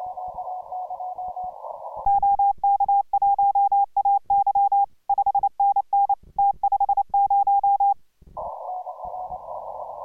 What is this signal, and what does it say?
Morse code, heard as a keyed tone, about 29 wpm, 810 Hz. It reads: OK1AY 5NN T50